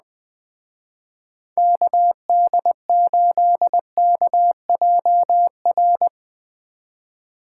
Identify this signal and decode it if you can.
Morse code, heard as a keyed tone, about 20 wpm, 700 Hz. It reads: KD8KJR
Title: Speed Racing - USA 2x3 Call Signs 17wpm